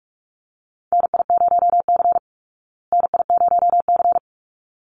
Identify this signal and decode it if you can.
Morse code, heard as a keyed tone, about 44 wpm, 700 Hz. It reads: NI0C NI0C
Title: Call Signs Repeated 3x - 45wpm